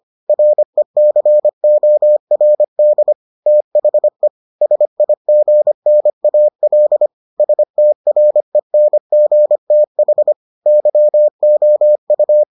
Morse code, heard as a keyed tone, about 25 wpm, 600 Hz. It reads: RECORD THE SIGNAL STRENGTH YOU